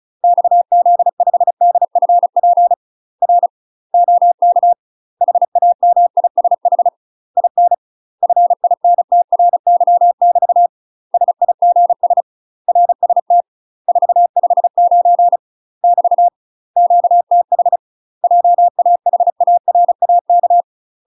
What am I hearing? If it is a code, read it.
Morse code, heard as a keyed tone, about 36 wpm, 700 Hz. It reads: XZ5DFP R OK HAMISH IN FINTRY <BT> SIGS RST 459 <BT> QTH JAHARAK
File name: UK MASTER QSOs TRAINING / 35WPM / UK MASTER QSOs 0038